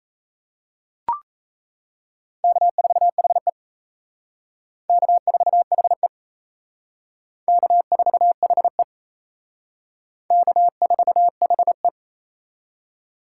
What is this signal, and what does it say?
Morse code, heard as a keyed tone, about 36 wpm, 700 Hz. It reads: K4HE K4HE K4HE K4HE